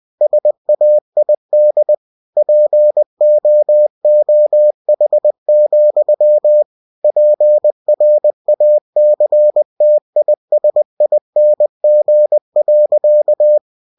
Morse code, heard as a keyed tone, about 20 wpm, 600 Hz. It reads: SAID POOH, PRACTISING.
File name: cw-books / winnie-the-pooh / winnie-20wpm-page-0061